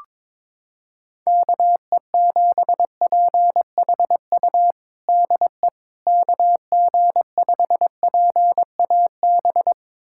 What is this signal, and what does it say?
Morse code, heard as a keyed tone, about 21 wpm, 700 Hz. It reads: KE7PHU DE KG5PAB
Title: USA Callsign DE Callsign - 6 Characters - 22wpm